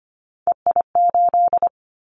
Morse code, heard as a keyed tone, about 25 wpm, 700 Hz. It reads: EI8
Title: Callsign Prefixes for Top 50 DXCC - 25wpm